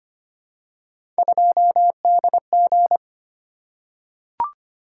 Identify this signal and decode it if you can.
Morse code, heard as a keyed tone, about 25 wpm, 700 Hz. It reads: 2DG
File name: Three Letter-Number Combinations - 25wpm